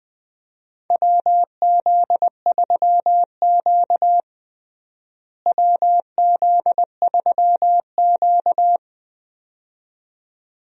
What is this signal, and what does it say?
Morse code, heard as a keyed tone, about 20 wpm, 700 Hz. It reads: WZ3Q WZ3Q
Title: USA Call Signs - 4 Characters - Repeated 2x - 20wpm